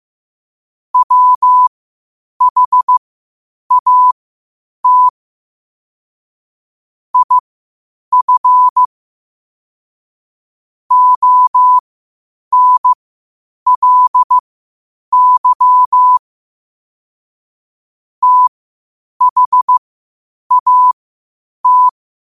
Morse code, heard as a keyed tone, about 15 wpm, spaced as for 8 wpm, 1000 Hz. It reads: WHAT IF ONLY THAT
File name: qotdfiles / 8wpm